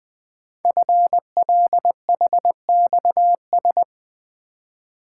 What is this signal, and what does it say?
Morse code, heard as a keyed tone, about 20 wpm, 700 Hz. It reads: FLHXS
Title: Code Groups - 5 Letters - 20wpm